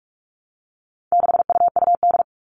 Morse code, heard as a keyed tone, about 44 wpm, 700 Hz. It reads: 6UUD
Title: All Cut Numbers - 4-Digit 45wpm